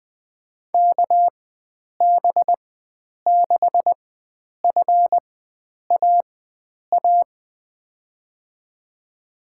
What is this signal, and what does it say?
Morse code, heard as a keyed tone, about 21 wpm, spaced as for 9 wpm, 700 Hz. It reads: KB6FAA
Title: USA Call Signs - 6 Characters - 4x Character Spacing - 20wpm